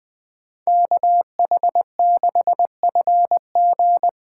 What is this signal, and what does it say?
Morse code, heard as a keyed tone, about 24 wpm, 700 Hz. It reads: KH6FG
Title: Speed Racing - USA 2x2 Call Signs 20wpm